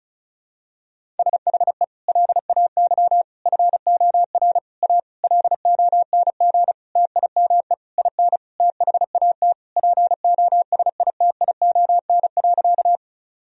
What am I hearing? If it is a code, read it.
Morse code, heard as a keyed tone, about 36 wpm, 700 Hz. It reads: SHE LAY FOR A LONG TIME IN THAT POSITION.
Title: Sentences from Top 1000 Words - Review Part 2 - 35wpm